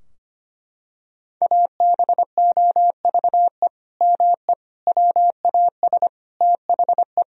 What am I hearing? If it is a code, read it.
Morse code, heard as a keyed tone, about 25 wpm, 700 Hz. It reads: ABOVE ME WAS THE
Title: Sentences from Top 1000 Words - Review Part 3 - 25wpm